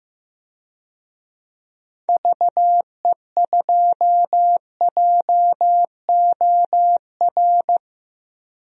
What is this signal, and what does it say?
Morse code, heard as a keyed tone, about 15 wpm, 700 Hz. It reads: VE2JOR
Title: DXCC-100 Call Signs - 6 Characters - Repeated 2x - 15wpm